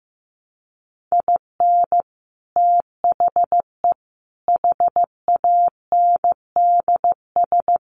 Morse code, heard as a keyed tone, about 15 wpm, 700 Hz. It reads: IN THE HANDS
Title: Sets of 4 Words 15wpm